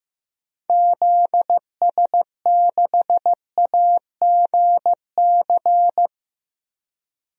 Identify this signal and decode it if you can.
Morse code, heard as a keyed tone, about 15 wpm, 700 Hz. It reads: ZS6AGC
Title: DXCC-100 Call Signs - 6 Characters - 15wpm